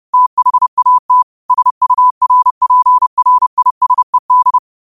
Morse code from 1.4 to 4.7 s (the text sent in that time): SURPRISED